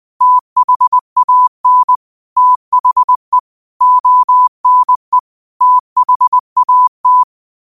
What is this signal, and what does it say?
Morse code, heard as a keyed tone, about 20 wpm, 1000 Hz. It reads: THAN THE ONE THAT
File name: qotdfiles / 20wpm